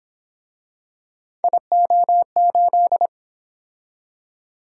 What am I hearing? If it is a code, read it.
Morse code, heard as a keyed tone, about 29 wpm, 700 Hz. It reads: IO8